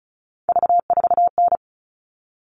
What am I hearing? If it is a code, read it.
Morse code, heard as a keyed tone, about 34 wpm, 700 Hz. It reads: V4N
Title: All Cut Numbers - 3-Digit 35wpm